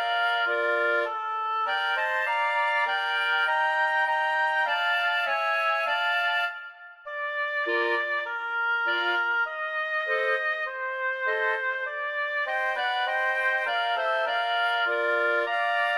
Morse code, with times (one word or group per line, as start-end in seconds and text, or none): none